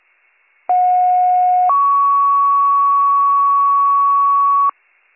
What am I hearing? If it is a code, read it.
Two-tone page, heard as a one-second tone, then a three-second tone, about 730 then 1120 Hz. It is heard from 0.7 to 4.7 s.